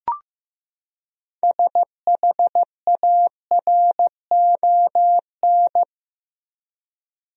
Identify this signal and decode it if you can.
Morse code, heard as a keyed tone, about 15 wpm, 700 Hz. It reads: SHARON
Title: USA Common Names - 15wpm